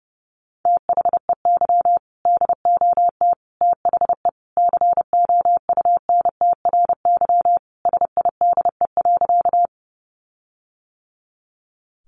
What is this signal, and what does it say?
Morse code, heard as a keyed tone, about 30 wpm, 700 Hz. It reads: THEY DOT THE COUNTRY SIDE.
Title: Sentences from Top 1500 Words 30wpm